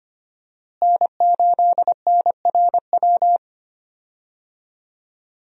Morse code, heard as a keyed tone, about 25 wpm, 700 Hz. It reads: N8NRW
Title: USA Call Signs - 5 Characters - Repeated 2x - 25wpm